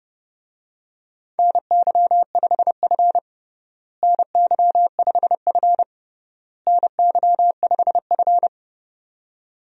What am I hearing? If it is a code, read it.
Morse code, heard as a keyed tone, about 30 wpm, 700 Hz. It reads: NY5F NY5F NY5F
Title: USA Call Signs - 4 Characters - Repeated 3x - 30wpm